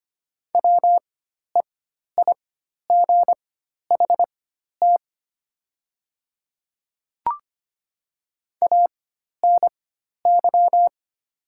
Morse code, heard as a keyed tone, about 25 wpm, spaced as for 12 wpm, 700 Hz. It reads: WEIGHT ANY
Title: 4x Character Spacing - Top 500 Words - 25wpm